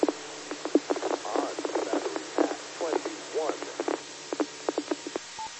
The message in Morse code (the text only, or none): none